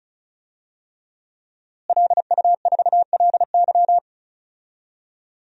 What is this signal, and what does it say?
Morse code, heard as a keyed tone, about 36 wpm, 700 Hz. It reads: LU4LY
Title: DXCC-100 Call Signs - 5 Characters - Repeated 3x - 35wpm